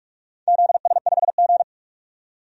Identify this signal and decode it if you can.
Morse code, heard as a keyed tone, about 44 wpm, 700 Hz. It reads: ZS5G
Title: DXCC-100 Call Signs - 4 Characters - 45wpm